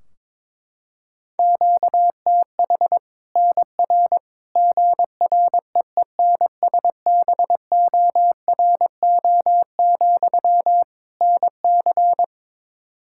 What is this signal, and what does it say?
Morse code, heard as a keyed tone, about 21 wpm, 700 Hz. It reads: QTH NR GREENSBORO, NC
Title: Practice QSO - Exchange 1 - Beginning - 22wpm